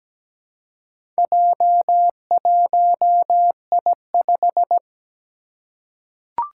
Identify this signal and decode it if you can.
Morse code, heard as a keyed tone, about 17 wpm, 700 Hz. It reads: J1I5
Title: Four Challenging Letter-Number Combinations 17wpm